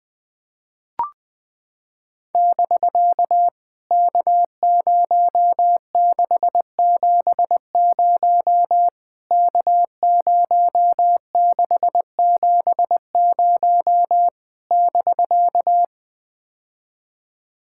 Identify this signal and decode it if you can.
Morse code, heard as a keyed tone, about 20 wpm, 700 Hz. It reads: <BK> K0670 K0670 <BK>
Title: POTA QSO Practice - 20wpm